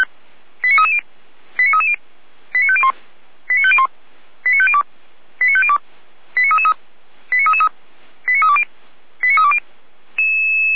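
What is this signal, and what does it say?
Five-tone selective call (ZVEI), heard as quick runs of five tones, tones of about 70 ms. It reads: 89330 89330 89501 89501 89502 89502 89333 89333 89320 89320
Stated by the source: ZVEI five-tone selective call received by SDR, from the Feuerwehr Orlamünde fire brigade.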